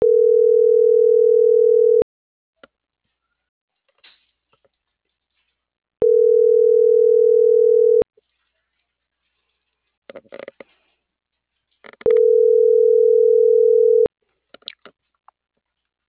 Ringback tone (North America), 440 + 480 Hz.
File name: CENTRAL MAERSK / out yunuel.morales - .32551